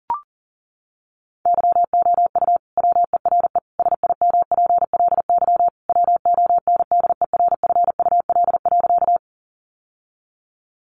Morse code, heard as a keyed tone, about 40 wpm, 700 Hz. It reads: YOU WERE SIMPLY WONDERFUL.
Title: Sentences from Top 1100 Words 40wpm